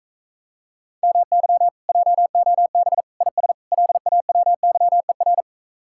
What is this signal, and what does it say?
Morse code, heard as a keyed tone, about 45 wpm, 700 Hz. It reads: MY JOB IS LAWYER